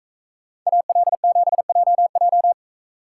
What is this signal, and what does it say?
Morse code, heard as a keyed tone, about 44 wpm, 700 Hz. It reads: AL7JJ